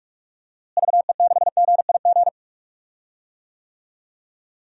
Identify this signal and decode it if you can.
Morse code, heard as a keyed tone, about 42 wpm, 700 Hz. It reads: VE6GIG
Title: DXCC-100 Call Signs - 6 Characters - Repeated 3x - 45wpm